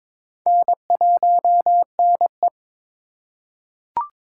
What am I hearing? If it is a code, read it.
Morse code, heard as a keyed tone, about 22 wpm, 700 Hz. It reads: N1NE
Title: USA Call Signs - 4x Character Spacing - 22wpm